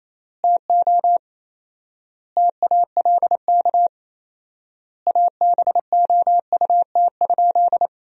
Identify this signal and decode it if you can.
Morse code, heard as a keyed tone, about 28 wpm, 700 Hz. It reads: TO TALK ABOUT?
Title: 4x Word-Spacing - Sentences from Top 400 Words - 28wpm